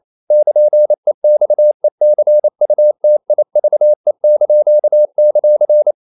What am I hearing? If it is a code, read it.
Morse code, heard as a keyed tone, about 28 wpm, 600 Hz. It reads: <KN>EXECUTIVE);
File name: CW / Practice / NationalCapitals28